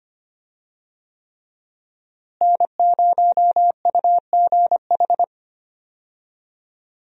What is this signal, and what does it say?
Morse code, heard as a keyed tone, about 25 wpm, 700 Hz. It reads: N0UGH
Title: USA Call Signs - 5 Characters - Repeated 3x - 25wpm